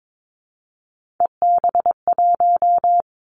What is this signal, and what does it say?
Morse code, heard as a keyed tone, about 22 wpm, 700 Hz. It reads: EB1
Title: Three Challenging Letter-Number Combinations 22wpm